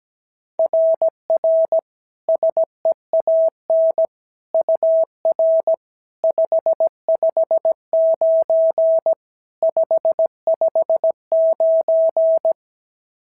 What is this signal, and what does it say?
Morse code, heard as a keyed tone, about 15 wpm, 650 Hz. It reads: RR SEAN UR 559 559
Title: SOTA QSO Practice 17wpm